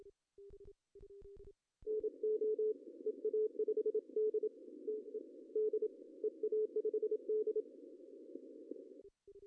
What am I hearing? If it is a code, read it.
Morse code, heard as a keyed tone, about 29 wpm, 430 Hz. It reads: NO EA5D TED EA5D
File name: CW / cqww-cw-k5tr